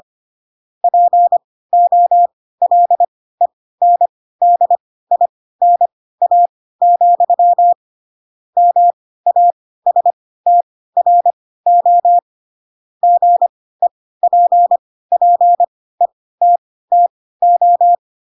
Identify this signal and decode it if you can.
Morse code, heard as a keyed tone, about 25 wpm, spaced as for 16 wpm, 700 Hz. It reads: POLENDINA, MASTRO GEPPETTO
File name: Pinocchio / Pinocchio ch2 10 25 ik1hge